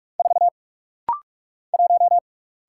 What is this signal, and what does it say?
Morse code, heard as a keyed tone, about 43 wpm, 700 Hz. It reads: 4 1
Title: Single Letter-Number - Rapid-Fire 45wpm